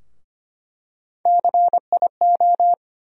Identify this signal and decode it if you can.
Morse code, heard as a keyed tone, about 25 wpm, 700 Hz. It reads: CIO
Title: USA Call Sign Postfixes - 3 Letters 25wpm